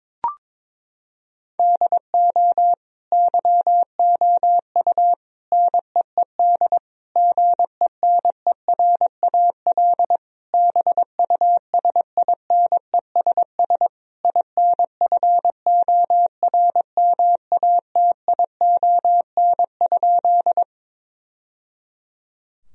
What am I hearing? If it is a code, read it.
Morse code, heard as a keyed tone, about 22 wpm, 700 Hz. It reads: DO YOU NEED GENERAL BUSINESS INFORMATION?
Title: Sentences from Top 1000 Words - Review Part 3 - 22wpm